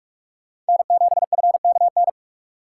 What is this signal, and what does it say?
Morse code, heard as a keyed tone, about 46 wpm, 700 Hz. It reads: N7FKN
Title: USA Call Signs - 5 Characters - 4x Character Spacing - 45wpm